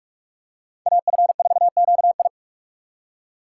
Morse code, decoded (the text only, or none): AF4QI